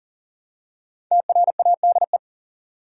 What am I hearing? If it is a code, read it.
Morse code, heard as a keyed tone, about 40 wpm, 700 Hz. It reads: TRADE
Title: Five Letter Words - 40wpm